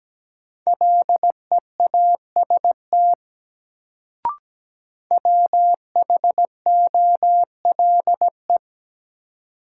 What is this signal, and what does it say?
Morse code, heard as a keyed tone, about 17 wpm, 700 Hz. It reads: LEAST WHOLE
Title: Sets of 2 Words - New Word Drill - 17wpm